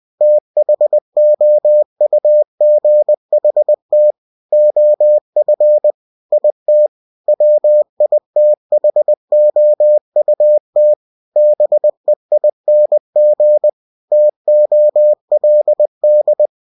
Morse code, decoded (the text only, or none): THOUGHT OF IT WITHOUT BEING TOLD